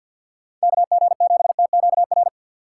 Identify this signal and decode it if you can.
Morse code, heard as a keyed tone, about 51 wpm, 700 Hz. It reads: KG7TQR
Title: USA Call Signs - 6 Characters - 6x Character Spacing - 50wpm